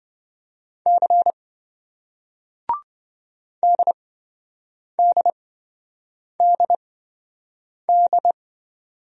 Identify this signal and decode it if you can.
Morse code, heard as a keyed tone, about 27 wpm, spaced as for 7 wpm, 700 Hz. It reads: C DDDD